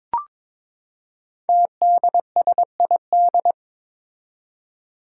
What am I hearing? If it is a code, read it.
Morse code, heard as a keyed tone, about 23 wpm, 700 Hz. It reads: TDSID
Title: Five Challenging Letter-Number Combinations 22wpm